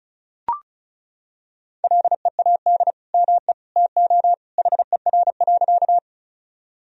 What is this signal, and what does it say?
Morse code, heard as a keyed tone, about 36 wpm, 700 Hz. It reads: LEAD ME TO HER.
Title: Sentences from Top 1000 Words - Review Part 2 - 35wpm